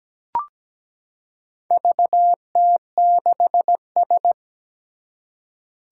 Morse code, heard as a keyed tone, about 17 wpm, 700 Hz. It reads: VT6S
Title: Four Challenging Letter-Number Combinations 17wpm